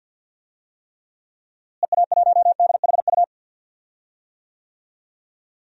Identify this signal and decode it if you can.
Morse code, heard as a keyed tone, about 51 wpm, 700 Hz. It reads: EA1DHU
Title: DXCC-100 Call Signs - 6 Characters - Repeated 3x - 50wpm